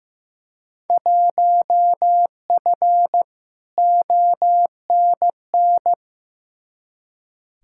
Morse code, heard as a keyed tone, about 16 wpm, 700 Hz. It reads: 1F ONN
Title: Field Day Exchanges Encore - 15wpm